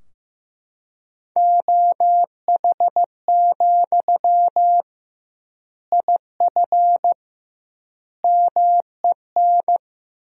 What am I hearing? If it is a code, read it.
Morse code, heard as a keyed tone, about 15 wpm, 700 Hz. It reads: OH, IF MEN
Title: 2x Word-Spacing - Sentences from Top 500 Words - 15wpm